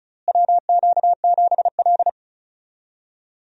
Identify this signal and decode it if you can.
Morse code, heard as a keyed tone, about 36 wpm, 700 Hz. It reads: WQ7L
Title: USA Call Signs - 4 Characters - Repeated 2x - 35wpm